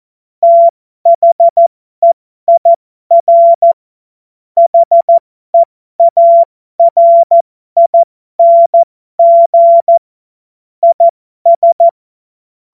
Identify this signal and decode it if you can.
Morse code, heard as a keyed tone, about 14 wpm, spaced as for 12 wpm, 680 Hz. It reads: THEIR HEARING IS
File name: Morsekurs / Witze